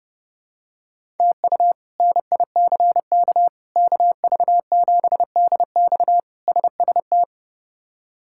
Morse code, heard as a keyed tone, about 30 wpm, 700 Hz. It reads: TU NICK KV7DX SST
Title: K1USN SST QSO Practice - 30wpm